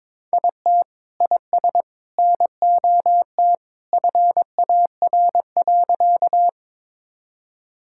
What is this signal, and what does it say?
Morse code, heard as a keyed tone, about 22 wpm, 700 Hz. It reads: IT IS NOT FAR.
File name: practice / 22wpm / Sentences with 2-Word Phrases - 22wpm